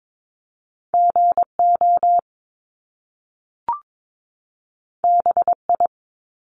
Morse code, heard as a keyed tone, about 22 wpm, 700 Hz. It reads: GO BI
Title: Top Two Letter Combinations in Words 22wpm